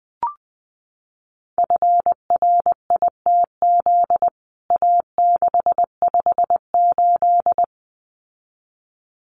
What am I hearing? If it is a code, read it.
Morse code, heard as a keyed tone, about 20 wpm, 700 Hz. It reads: FRITZ A658
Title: CWT Exchange 20wpm